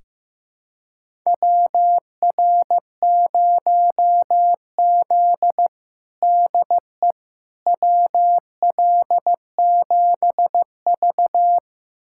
Morse code, heard as a keyed tone, about 15 wpm, 700 Hz. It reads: WR0Z DE WL7V
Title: USA Callsign DE Callsign - 4 Characters - 15wpm